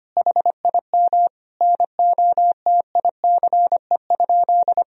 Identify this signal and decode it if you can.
Morse code, heard as a keyed tone, about 25 wpm, 700 Hz. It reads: HIM NOTICE?
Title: Sentences from Top 1000 Words - Review Part 2 - 25wpm